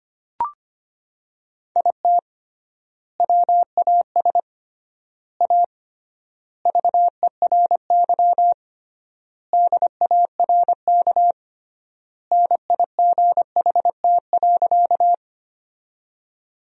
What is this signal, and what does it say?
Morse code, heard as a keyed tone, about 25 wpm, 700 Hz. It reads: IT WAS A VERY DARK NIGHT.
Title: 3x Word-Spacing - Sentences from Top 500 Words - 25wpm